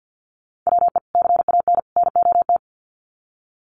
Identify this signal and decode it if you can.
Morse code, heard as a keyed tone, about 50 wpm, 700 Hz. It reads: WE CAN NOT
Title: Sets of 3 Words 50wpm